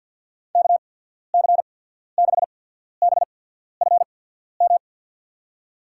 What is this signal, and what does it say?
Morse code, heard as a keyed tone, about 54 wpm, spaced as for 14 wpm, 690 Hz. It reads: KC6BFM
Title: USA Call Signs - 6 Characters - 8x Character Spacing - 50wpm